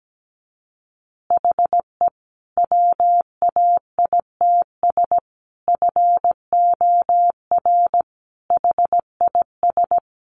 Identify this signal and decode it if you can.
Morse code, heard as a keyed tone, about 17 wpm, 700 Hz. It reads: HE WAITS FOR HIS
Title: Additional Easy Phrases 17wpm